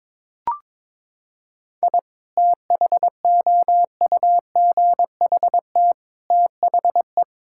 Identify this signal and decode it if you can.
Morse code, heard as a keyed tone, about 21 wpm, 700 Hz. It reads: I THOUGHT THE
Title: Sentences from Top 1000 Words - Review Part 5 - 22wpm